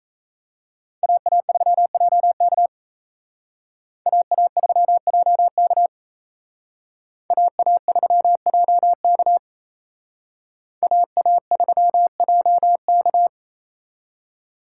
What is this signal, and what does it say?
Morse code, heard as a keyed tone, about 35 wpm, 700 Hz. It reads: AA3JK AA3JK AA3JK AA3JK